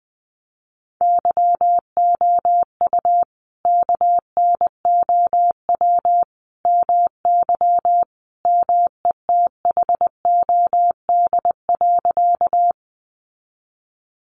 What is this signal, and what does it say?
Morse code, heard as a keyed tone, about 19 wpm, 700 Hz. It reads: YOU KNOW MY METHOD.
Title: Sentences from Top 600 Words 20wpm